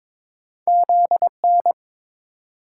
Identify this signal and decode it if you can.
Morse code, heard as a keyed tone, about 30 wpm, 700 Hz. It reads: ZN